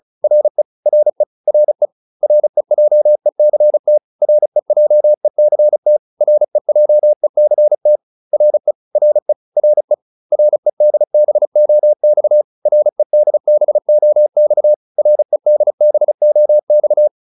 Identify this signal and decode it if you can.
Morse code, heard as a keyed tone, about 34 wpm, 600 Hz. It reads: RE RE RE REJECT REJECT REJECT RE RE RE REDBOX REDBOX REDBOX
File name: RE6 35